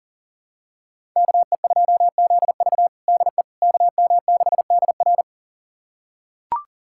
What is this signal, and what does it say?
Morse code, heard as a keyed tone, about 40 wpm, 700 Hz. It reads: KE2ZV DE KM6DR
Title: USA Callsign DE Callsign - 5 Characters - 40wpm